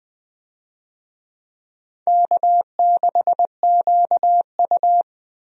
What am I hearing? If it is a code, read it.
Morse code, heard as a keyed tone, about 20 wpm, 700 Hz. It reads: K6QU
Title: USA Call Signs - 4 Characters - 20wpm